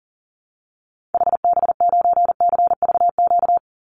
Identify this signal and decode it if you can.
Morse code, heard as a keyed tone, about 40 wpm, 700 Hz. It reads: HB9CVQ
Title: Call Signs Repeated 3x - 40wpm